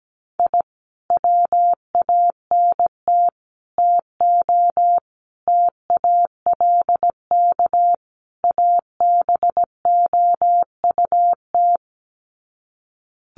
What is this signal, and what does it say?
Morse code, heard as a keyed tone, about 17 wpm, 700 Hz. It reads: I WANT TO TALK ABOUT
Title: Sets of 5 Words Encore 17wpm